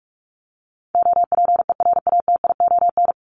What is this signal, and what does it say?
Morse code, heard as a keyed tone, about 44 wpm, 700 Hz. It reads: OPERATION
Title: Top 1501-1750 Words 45wpm